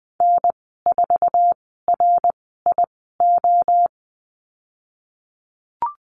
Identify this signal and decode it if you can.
Morse code, heard as a keyed tone, about 20 wpm, spaced as for 15 wpm, 700 Hz. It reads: N4RIO
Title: 055 Lesson 10g - Letter r - Call Signs 20wpm